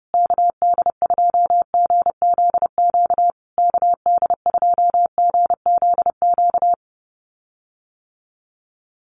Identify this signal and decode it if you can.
Morse code, heard as a keyed tone, about 30 wpm, 700 Hz. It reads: KD2GZQ KD2GZQ
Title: USA Call Signs Repeated 3x 30wpm